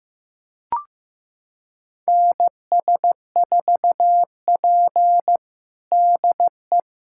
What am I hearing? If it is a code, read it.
Morse code, heard as a keyed tone, about 16 wpm, 700 Hz. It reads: NS4P DE
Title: USA Callsign DE Callsign - 4 Characters - 15wpm